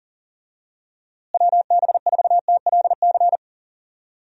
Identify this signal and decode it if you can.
Morse code, heard as a keyed tone, about 40 wpm, 700 Hz. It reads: WB4TLC
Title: USA Call Signs - 6 Characters - Repeated 2x - 40wpm